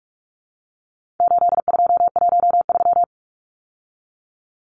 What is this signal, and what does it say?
Morse code, heard as a keyed tone, about 44 wpm, 700 Hz. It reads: 8213
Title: Numbers - 4-Digit 45wpm